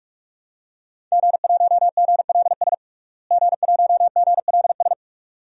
Morse code, heard as a keyed tone, about 42 wpm, 700 Hz. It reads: G1GLS G1GLS
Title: DXCC-100 Call Signs - 5 Characters - Repeated 2x - 45wpm